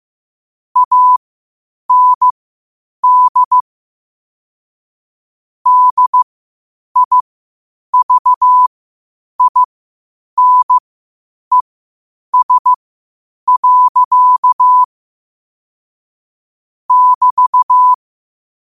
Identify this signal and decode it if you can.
Morse code, heard as a keyed tone, about 15 wpm, spaced as for 8 wpm, 1000 Hz. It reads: AND DIVINES. <BT>
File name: qotdfiles / 8wpm